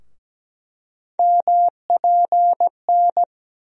Morse code, heard as a keyed tone, about 17 wpm, 700 Hz. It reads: MPN
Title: USA Call Sign Postfixes - 3 Letters 17wpm